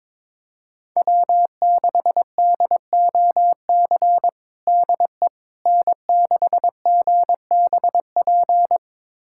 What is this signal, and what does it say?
Morse code, heard as a keyed tone, about 22 wpm, 700 Hz. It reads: W6DOC DE N6GBP
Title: USA Callsign DE Callsign - 5 Characters - 22wpm